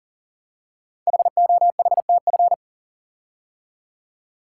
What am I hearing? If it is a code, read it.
Morse code, heard as a keyed tone, about 40 wpm, 700 Hz. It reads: HOHTF
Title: Code Groups - 5 Letters - 40wpm